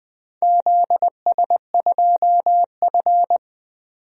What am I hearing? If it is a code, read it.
Morse code, heard as a keyed tone, about 20 wpm, 700 Hz. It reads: ZS2F
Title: DXCC-100 Call Signs - 4 Characters - 20wpm